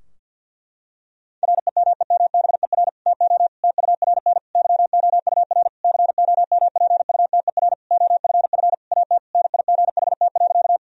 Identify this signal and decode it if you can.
Morse code, heard as a keyed tone, about 52 wpm, 700 Hz. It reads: REMEMBER TO TURN YOUR COMPUTER OFF AT NIGHT.
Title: Sentences from Top 1000 Words - Review Part 3 - 50wpm